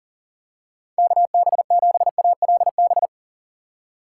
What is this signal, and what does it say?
Morse code, heard as a keyed tone, about 37 wpm, 700 Hz. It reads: KB7ALB
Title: USA Call Signs - 6 Characters - 6x Character Spacing - 40wpm